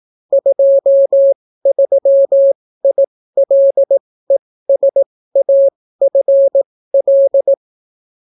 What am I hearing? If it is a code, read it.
Morse code, heard as a keyed tone, about 18 wpm, spaced as for 14 wpm, 550 Hz. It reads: 23ILESAFL